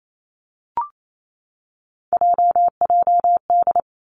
Morse code, heard as a keyed tone, about 28 wpm, 700 Hz. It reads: JJD
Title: Three Challenging Letter-Number Combinations 28wpm